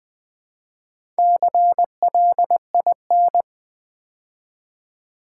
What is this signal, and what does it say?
Morse code, heard as a keyed tone, about 20 wpm, 700 Hz. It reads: CLIN
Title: Build A Word - 2 Letters At A Time - 8-Letter Words - 20wpm